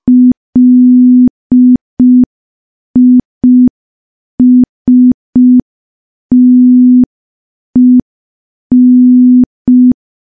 Morse code, heard as a keyed tone, about 5 wpm, 250 Hz. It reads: LISTEN
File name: morse / morse-training-05WPM